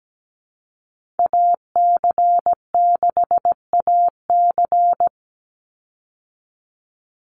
Morse code, heard as a keyed tone, about 17 wpm, 700 Hz. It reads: AC6AC
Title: Call Signs Repeated 3x - 17wpm